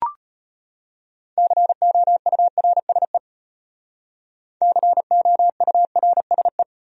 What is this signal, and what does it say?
Morse code, heard as a keyed tone, about 32 wpm, 700 Hz. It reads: COURSE COURSE